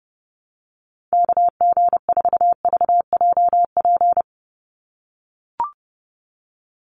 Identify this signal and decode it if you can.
Morse code, heard as a keyed tone, about 30 wpm, 700 Hz. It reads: KG4VJP
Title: USA Call Signs 30wpm